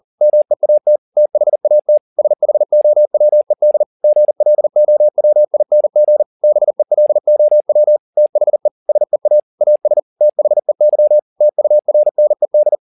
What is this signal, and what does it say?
Morse code, heard as a keyed tone, about 40 wpm, 600 Hz. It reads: MEAT THAT SHOWED GLOWING BELOW THE SEA AS THEY TURNED
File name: the old man and the sea 2 40wpm